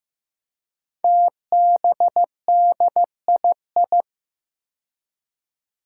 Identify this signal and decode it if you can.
Morse code, heard as a keyed tone, about 15 wpm, 700 Hz. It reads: TBDII
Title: Five Challenging Letter-Number Combinations 15wpm